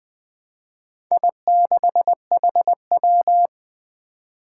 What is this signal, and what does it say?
Morse code, heard as a keyed tone, about 20 wpm, 700 Hz. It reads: I6HW
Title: Four Challenging Letter-Number Combinations 20wpm